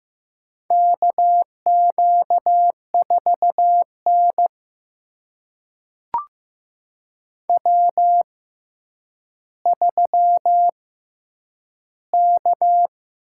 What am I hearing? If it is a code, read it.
Morse code, heard as a keyed tone, about 15 wpm, spaced as for 8 wpm, 700 Hz. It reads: KQ4N W3K
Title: USA Call Signs - 4 Characters - 6x Character Spacing - 15wpm